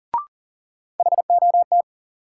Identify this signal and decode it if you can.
Morse code, heard as a keyed tone, about 40 wpm, 700 Hz. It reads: HOT
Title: Sentences from Top 500 Words - New Word Drill - 40wpm